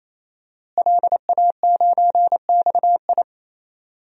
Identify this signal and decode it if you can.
Morse code, heard as a keyed tone, about 28 wpm, 700 Hz. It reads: LA9XI
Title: DXCC-100 Call Signs - 5 Characters - Repeated 3x - 28wpm